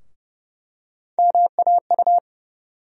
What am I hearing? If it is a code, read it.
Morse code, heard as a keyed tone, about 30 wpm, 700 Hz. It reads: MAU
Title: USA Call Sign Postfixes - 3 Letters 30wpm